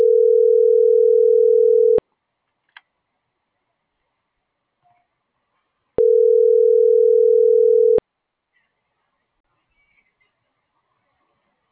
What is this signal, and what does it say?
Ringback tone (North America), 440 + 480 Hz.